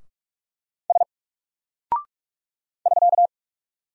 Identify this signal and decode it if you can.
Morse code, heard as a keyed tone, about 44 wpm, 700 Hz. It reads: S <SK>
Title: Single Character (Letter:Number:Punctuation:Prosign) 45wpm